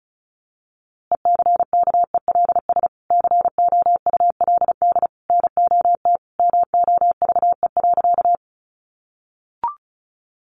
Morse code, heard as a keyed tone, about 34 wpm, 700 Hz. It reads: ECKELS COULD NOT MOVE.
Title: A Sound Of Thunder 35wpm